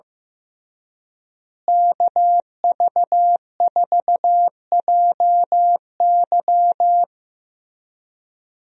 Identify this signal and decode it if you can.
Morse code, heard as a keyed tone, about 15 wpm, 700 Hz. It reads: KV4JY
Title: USA Call Signs - 5 Characters - Repeated 3x - 15wpm